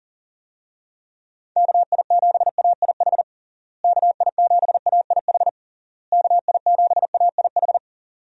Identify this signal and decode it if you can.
Morse code, heard as a keyed tone, about 40 wpm, 700 Hz. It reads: KI7AIH KI7AIH KI7AIH
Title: USA Call Signs - 6 Characters - Repeated 3x - 40wpm